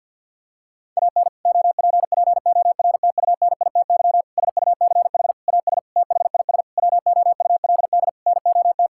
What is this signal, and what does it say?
Morse code, heard as a keyed tone, about 51 wpm, 700 Hz. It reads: AN OPPORTUNITY SUCH AS THIS WOULD NOT
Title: Sentences from Top 1600 Words 50wpm